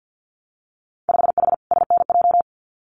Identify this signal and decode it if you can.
Morse code, heard as a keyed tone, about 50 wpm, 700 Hz. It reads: 5H SNJ
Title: Winter Field Day Exchanges Encore 50wpm